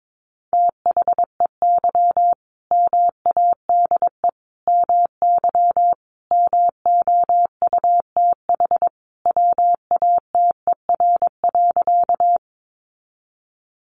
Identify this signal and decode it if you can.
Morse code, heard as a keyed tone, about 22 wpm, 700 Hz. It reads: THEY MADE MY MOUTH WATER.
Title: Sentences from Top 1500 Words 22wpm